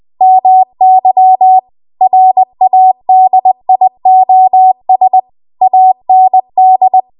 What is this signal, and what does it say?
Morse code, heard as a keyed tone, about 20 wpm, 750 Hz. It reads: MY RADIOS AND
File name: Morse / 20WPM